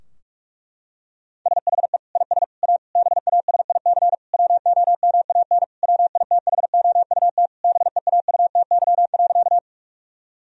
Morse code, heard as a keyed tone, about 46 wpm, 700 Hz. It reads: SHE IS A BASIC WOMAN WITHOUT BEAUTY.
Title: Sentences from Top 1000 Words - Review Part 3 - 45wpm